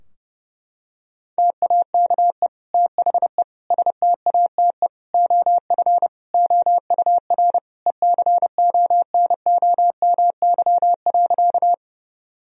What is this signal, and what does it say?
Morse code, heard as a keyed tone, about 30 wpm, 700 Hz. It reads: TAKE THE STATE OF OUR ECONOMY.
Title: Sentences from Top 1000 Words - Review Part 3 - 30wpm